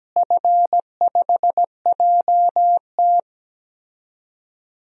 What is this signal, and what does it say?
Morse code, heard as a keyed tone, about 17 wpm, 700 Hz. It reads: F5JT
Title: DXCC-100 Call Signs - 4 Characters - Repeated 3x - 17wpm